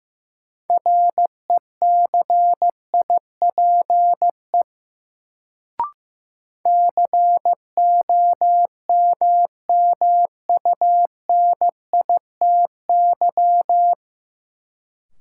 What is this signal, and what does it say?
Morse code, heard as a keyed tone, about 15 wpm, 700 Hz. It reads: RECIPE COMMUNITY
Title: Sentences from Top 1200 Words - New Word Drill Alt - 15wpm